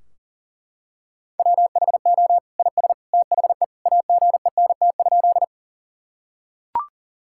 Morse code, heard as a keyed tone, about 40 wpm, 700 Hz. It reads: WHO IS THE AGENT?
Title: Sentences from Top 1600 Words 40wpm